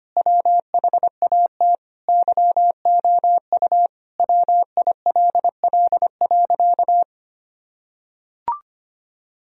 Morse code, heard as a keyed tone, about 25 wpm, 700 Hz. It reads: WHAT YOU WILL.
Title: Sentences from Top 1000 Words - Review Part 2 - 25wpm